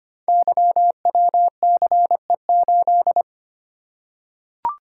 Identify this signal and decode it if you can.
Morse code, heard as a keyed tone, about 25 wpm, 700 Hz. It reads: YWCE8
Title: Code Groups - 5 Letter:Numbers - 25wpm